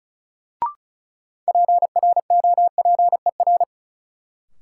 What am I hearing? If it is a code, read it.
Morse code, heard as a keyed tone, about 36 wpm, 700 Hz. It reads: PROPER
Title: Sentences from Top 800 Words - New Word Drill - 35wpm